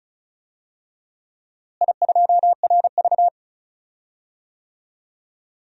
Morse code, heard as a keyed tone, about 36 wpm, 700 Hz. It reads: I2RV
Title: DXCC-100 Call Signs - 4 Characters - Repeated 2x - 35wpm